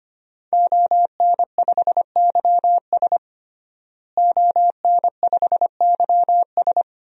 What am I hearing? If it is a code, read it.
Morse code, heard as a keyed tone, about 25 wpm, 700 Hz. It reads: ON5YS ON5YS
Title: DXCC-100 Call Signs - 5 Characters - Repeated 3x - 25wpm